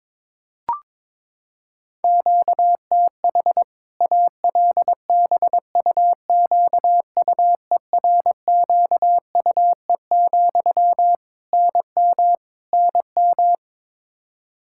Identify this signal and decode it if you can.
Morse code, heard as a keyed tone, about 22 wpm, 700 Hz. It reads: QTH ALBUQUERQUE, NM NM
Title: K1USN SST QSO Practice 22wpm